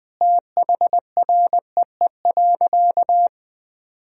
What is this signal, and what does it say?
Morse code, heard as a keyed tone, about 20 wpm, 700 Hz. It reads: THREE.
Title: Sentences from Top 1000 Words - Review Part 2 - 20wpm